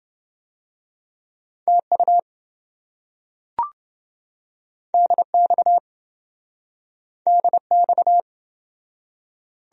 Morse code, heard as a keyed tone, about 27 wpm, 700 Hz. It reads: TU DX DX